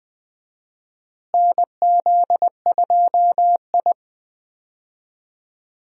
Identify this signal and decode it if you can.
Morse code, heard as a keyed tone, about 20 wpm, 700 Hz. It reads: NZ2I
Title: USA Call Signs - 4 Characters - Repeated 2x - 20wpm